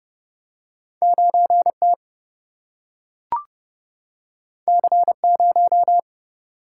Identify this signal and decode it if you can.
Morse code, heard as a keyed tone, about 30 wpm, 700 Hz. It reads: 9T C0
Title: Single-Letter Single-Number 30wpm